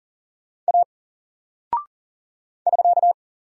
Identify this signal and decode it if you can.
Morse code, heard as a keyed tone, about 40 wpm, 700 Hz. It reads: A <SK>
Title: Single Character (Letter:Number:Punctuation:Prosign) 40wpm